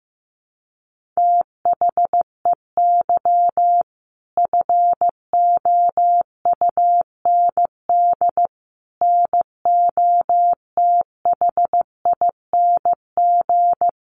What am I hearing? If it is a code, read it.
Morse code, heard as a keyed tone, about 15 wpm, 700 Hz. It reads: THEY FOUND NOTHING
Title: Sentences from Top 1500 Words 15wpm